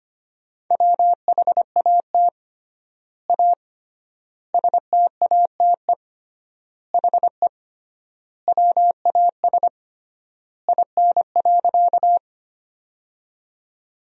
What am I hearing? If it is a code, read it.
Morse code, heard as a keyed tone, about 25 wpm, 700 Hz. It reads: WHAT A STATE HE WAS IN.
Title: 3x Word-Spacing - Sentences from Top 400 Words - 25wpm